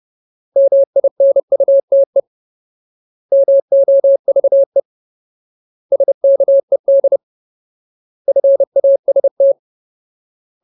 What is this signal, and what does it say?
Morse code, heard as a keyed tone, about 30 wpm, 550 Hz. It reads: MINUTE MOVE SKED FAST
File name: Ultimate Word List / 30WPM Ultimate